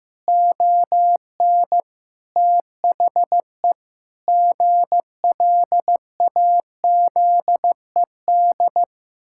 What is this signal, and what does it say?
Morse code, heard as a keyed tone, about 15 wpm, 700 Hz. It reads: ON THE GLAZED
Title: Pangram Challenge v2 - 15wpm